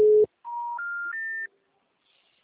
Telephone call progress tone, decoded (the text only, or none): special information tone